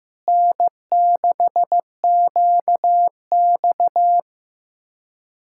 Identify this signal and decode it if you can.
Morse code, heard as a keyed tone, about 16 wpm, 700 Hz. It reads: N6QX
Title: Speed-Racing Common QSO Phrases Encore - 15wpm